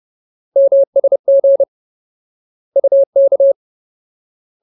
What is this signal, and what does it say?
Morse code, heard as a keyed tone, about 30 wpm, 550 Hz. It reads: MSG UK
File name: Ultimate Word List / 30WPM Ultimate